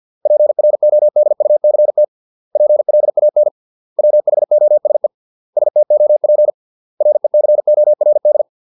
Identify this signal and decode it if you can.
Morse code, heard as a keyed tone, about 50 wpm, 600 Hz. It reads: PRODUCT PLAN WHOSE STOP RECORD